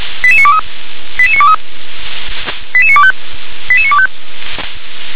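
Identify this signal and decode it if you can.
Five-tone selective call (ZVEI), heard as quick runs of five tones, tones of about 70 ms. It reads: 88923 88923 88925 88925
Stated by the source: ZVEI five-tone selective call received on SDR.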